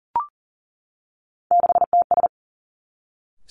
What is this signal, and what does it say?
Morse code, heard as a keyed tone, about 39 wpm, 700 Hz. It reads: 6TS
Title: Three Challenging Letter-Number Combinations 40wpm